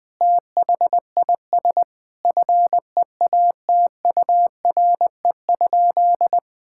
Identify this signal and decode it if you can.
Morse code, heard as a keyed tone, about 20 wpm, 700 Hz. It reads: THIS FEATURE?
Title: Sentences from Top 1000 Words - Review Part 5 - 20wpm